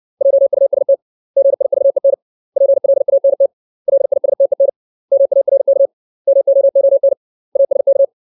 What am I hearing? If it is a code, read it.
Morse code, heard as a keyed tone, about 60 wpm, 550 Hz. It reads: JUST GIVEN PLANT LISTEN MARK MOON ASK
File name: 632 words / 632 Words 60WPM